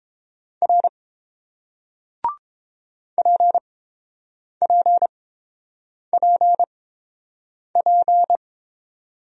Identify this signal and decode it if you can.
Morse code, heard as a keyed tone, about 31 wpm, spaced as for 7 wpm, 700 Hz. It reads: R PPPP